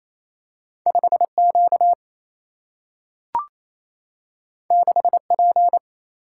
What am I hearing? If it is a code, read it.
Morse code, heard as a keyed tone, about 28 wpm, 700 Hz. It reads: 5Q 6P
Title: Single-Letter Single-Number 28wpm